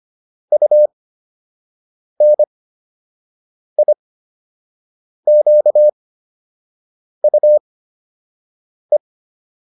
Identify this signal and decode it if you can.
Morse code, heard as a keyed tone, about 25 wpm, spaced as for 6 wpm, 600 Hz. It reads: UNIQUE